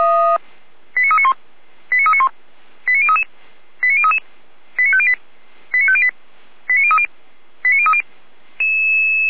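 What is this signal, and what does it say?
Five-tone selective call (ZVEI), heard as quick runs of five tones, tones of about 70 ms. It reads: 89391 89391 89033 89033 89508 89508 89030 89030
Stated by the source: Five-tone ZVEI selective call 89140 received by SDR, from the audio server of Feuerwehr Orlamünde.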